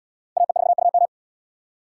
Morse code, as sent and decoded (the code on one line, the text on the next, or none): .- . ....- ..- --
AE4UM